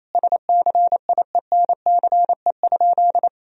Morse code, heard as a keyed tone, about 28 wpm, 700 Hz. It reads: SCIENCE?
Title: Sentences from Top 1000 Words - Review Part 2 - 28wpm